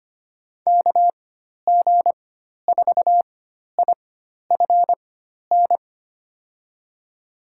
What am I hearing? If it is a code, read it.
Morse code, heard as a keyed tone, about 25 wpm, spaced as for 12 wpm, 700 Hz. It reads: KG4IFN
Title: USA Call Signs - 4x Character Spacing - 25wpm